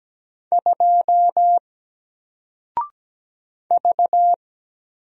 Morse code, heard as a keyed tone, about 17 wpm, 700 Hz. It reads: 2 V
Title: Single Character (Letter:Number:Punctuation:Prosign) 17wpm